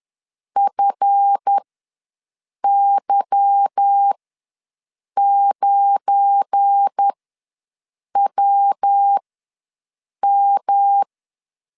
Morse code, heard as a keyed tone, about 10 wpm, spaced as for 6 wpm, 800 Hz. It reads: FY9WM
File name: LEZIONI / 8 YJ45 / 8 YJ ws 10wpm 4043